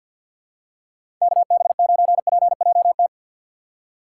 Morse code, heard as a keyed tone, about 51 wpm, 690 Hz. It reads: KB9PJT
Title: USA Call Signs - 6x Character Spacing - 50wpm